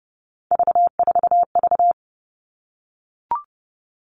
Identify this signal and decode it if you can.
Morse code, heard as a keyed tone, about 30 wpm, 700 Hz. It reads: V4V
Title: All Cut Numbers - 3-Digit 30wpm